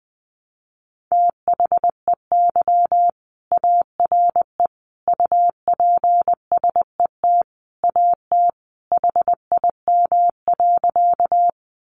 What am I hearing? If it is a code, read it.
Morse code, heard as a keyed tone, about 20 wpm, 700 Hz. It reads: THEY ARE UPSET AT HIM.
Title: Sentences from Top 800 Words 20wpm